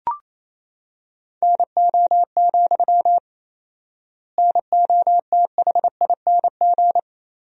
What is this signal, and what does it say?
Morse code, heard as a keyed tone, about 28 wpm, 700 Hz. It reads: NO, NOTHING
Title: 4x Word-Spacing - Sentences from Top 500 Words - 28wpm